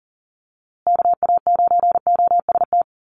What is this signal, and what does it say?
Morse code, heard as a keyed tone, about 40 wpm, 700 Hz. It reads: KA9OST
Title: USA Call Signs Repeated 2x 40wpm